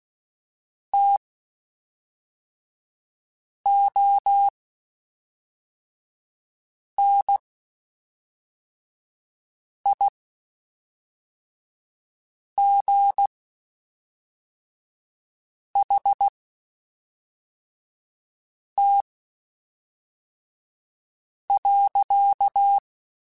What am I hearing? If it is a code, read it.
Morse code, heard as a keyed tone, about 16 wpm, spaced as for 3 wpm, 800 Hz. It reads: TONIGHT.
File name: (spaced) / QSOs Spaced-mp3s / WD0BGP 1 @5 sp